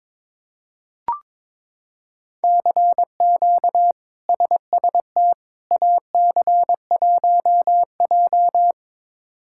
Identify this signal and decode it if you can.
Morse code, heard as a keyed tone, about 22 wpm, 700 Hz. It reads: CQ SST AC1J
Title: K1USN SST QSO Practice - 22wpm